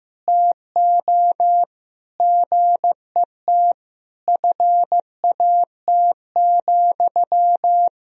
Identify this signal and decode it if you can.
Morse code, heard as a keyed tone, about 15 wpm, 700 Hz. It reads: TO GET FAT,
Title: Sentences from Top 1000 Words - Review Part 4 - 15wpm